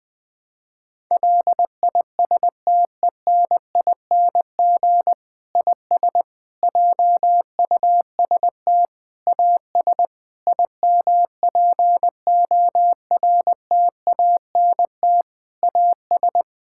Morse code, heard as a keyed tone, about 21 wpm, 700 Hz. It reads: LISTENING IS JUST AS IMPORTANT AS
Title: Difficult Sentences with Hints - 20wpm